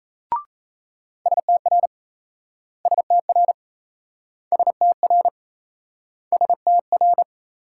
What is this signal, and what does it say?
Morse code, heard as a keyed tone, about 38 wpm, 700 Hz. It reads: STR STR STR STR